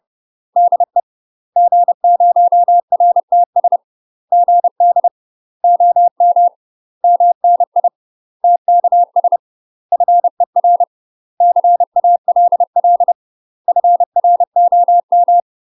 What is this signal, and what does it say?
Morse code, heard as a keyed tone, about 30 wpm, 700 Hz. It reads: DE G0RTS GD OM MNI TKS FER CALL FROM
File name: UK MASTER QSOs TRAINING / 30WPM / UK MASTER QSOs 0044